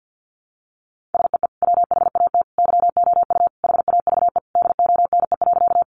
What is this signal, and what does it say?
Morse code, heard as a keyed tone, about 50 wpm, 700 Hz. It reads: SEE WHAT YOU HAVE DONE.
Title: Sentences from Top 500 Words 50wpm